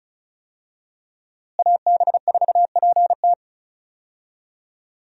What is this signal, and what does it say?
Morse code, heard as a keyed tone, about 36 wpm, 700 Hz. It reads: AB4PT
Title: USA Call Signs - 5 Characters - Repeated 3x - 35wpm